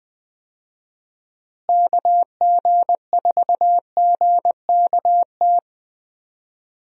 Morse code, heard as a keyed tone, about 21 wpm, 700 Hz. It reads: KG4GKT
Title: USA Call Signs - 6 Characters - Repeated 2x - 20wpm